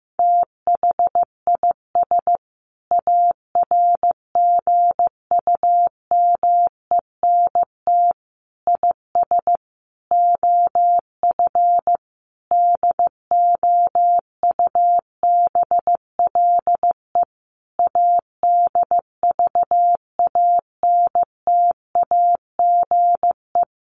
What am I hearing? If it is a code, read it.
Morse code, heard as a keyed tone, about 15 wpm, 700 Hz. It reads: THIS ARGUMENT IS OF DOUBLE ADVANTAGE
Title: Sentences from Top 1400 Words 15wpm